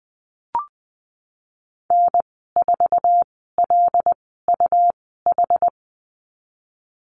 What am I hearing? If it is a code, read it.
Morse code, heard as a keyed tone, about 19 wpm, spaced as for 14 wpm, 700 Hz. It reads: N4LUH
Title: 147 Lesson 24g - Letter y - Call Signs 20wpm